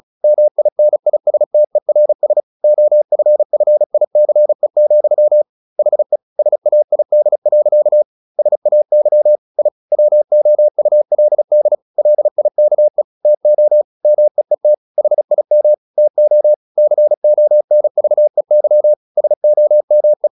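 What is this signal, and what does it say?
Morse code, heard as a keyed tone, about 37 wpm, 600 Hz. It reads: MINISTERS OFFICE, HE SAID. SAY I WOULD LIKE TO MEET HIM TO CONVEY SOME
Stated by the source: Morse practice at 35 wpm from CW med Gnister, sending Ken Follett's novel Never in English.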